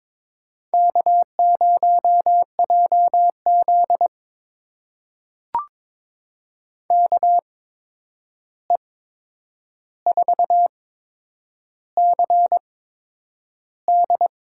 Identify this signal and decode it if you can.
Morse code, heard as a keyed tone, about 21 wpm, spaced as for 9 wpm, 700 Hz. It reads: K0JZ KE4CD
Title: USA Call Signs - 8x Character Spacing - 22wpm